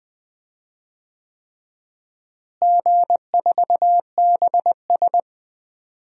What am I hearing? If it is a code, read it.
Morse code, heard as a keyed tone, about 20 wpm, 700 Hz. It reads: G4BS
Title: DXCC-100 Call Signs - 4 Characters - Repeated 3x - 20wpm